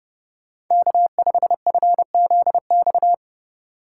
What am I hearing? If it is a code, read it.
Morse code, heard as a keyed tone, about 35 wpm, 700 Hz. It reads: K5FZX
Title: Speed Racing - USA 1x3 Call Signs 30wpm